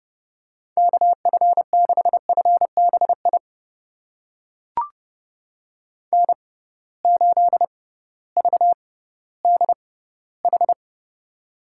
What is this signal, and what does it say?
Morse code, heard as a keyed tone, about 30 wpm, spaced as for 16 wpm, 700 Hz. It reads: KF6FBI N8VDH
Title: USA Call Signs - 6x Character Spacing - 30wpm